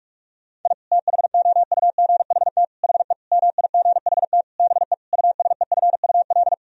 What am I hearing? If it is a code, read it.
Morse code, heard as a keyed tone, about 42 wpm, 700 Hz. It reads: I THOUGHT HE MIGHT BE USEFUL
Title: Sentences from Top 1000 Words - Review Part 4 - 45wpm